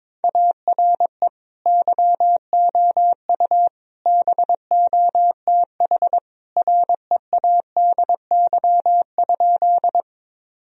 Morse code, heard as a keyed tone, about 22 wpm, 700 Hz. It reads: ARE YOU BOTH READY?
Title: Sentences from Top 1000 Words - Review Part 1 - 22wpm